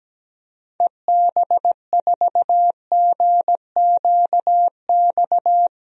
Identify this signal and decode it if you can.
Morse code, heard as a keyed tone, about 17 wpm, 700 Hz. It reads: EB4GQX
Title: DXCC-100 Call Signs - 6 Characters - Repeated 2x - 17wpm